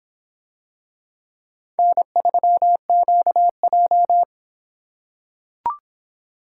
Morse code, heard as a keyed tone, about 24 wpm, 700 Hz. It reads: N3QJ